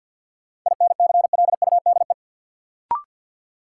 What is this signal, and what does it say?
Morse code, heard as a keyed tone, about 53 wpm, 700 Hz. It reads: INCLUDE
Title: Sentences from Top 1000 Words - Review Part 3 - New Word Drill - 50wpm